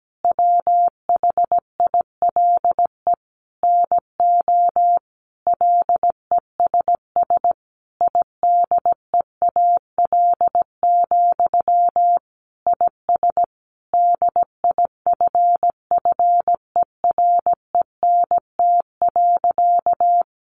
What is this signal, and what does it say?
Morse code, heard as a keyed tone, about 17 wpm, 700 Hz. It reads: WHILE NO LESS IDEAL, IS DIFFERENT.
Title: Sentences from Top 1400 Words 17wpm